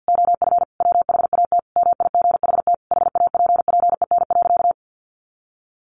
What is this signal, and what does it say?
Morse code, heard as a keyed tone, about 50 wpm, 700 Hz. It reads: OF WHAT MIGHT HAPPEN.
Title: Sentences from Top 400 Words 50wpm